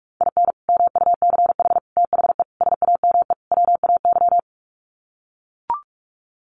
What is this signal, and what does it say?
Morse code, heard as a keyed tone, about 44 wpm, 700 Hz. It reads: IN MUCH THE SAME WAY
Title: Sets of 5 Words Encore 45wpm